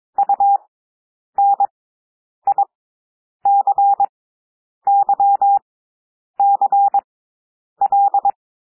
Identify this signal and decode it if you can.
Morse code, heard as a keyed tone, about 22 wpm, spaced as for 9 wpm, 800 Hz. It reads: UNICYCL
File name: cwmp3 / W2 SOUND WORDS 22 wpm